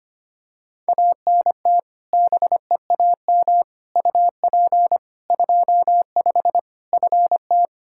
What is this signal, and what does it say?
Morse code, heard as a keyed tone, about 25 wpm, 700 Hz. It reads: ANT BEAM UP 25 FT
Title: K1USN SST QSO Practice 25wpm